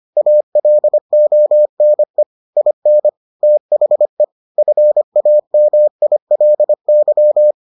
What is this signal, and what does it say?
Morse code, heard as a keyed tone, about 25 wpm, 600 Hz. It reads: ALONE IN THE FAMILY